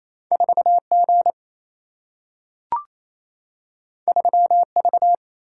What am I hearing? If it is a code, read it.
Morse code, heard as a keyed tone, about 28 wpm, 700 Hz. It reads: 4G 3V
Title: Single-Letter Single-Number 28wpm